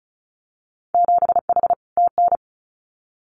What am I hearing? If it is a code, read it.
Morse code, heard as a keyed tone, about 36 wpm, 700 Hz. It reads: 7H TN
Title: Winter Field Day Exchanges Encore 35wpm